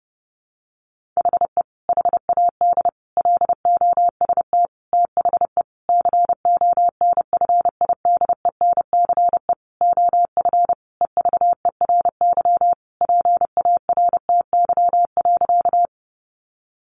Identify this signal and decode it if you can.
Morse code, heard as a keyed tone, about 30 wpm, 700 Hz. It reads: HE HAD LOST THE CONFIDENCE OF EVERY PARTY.
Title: Sentences from Top 1500 Words 30wpm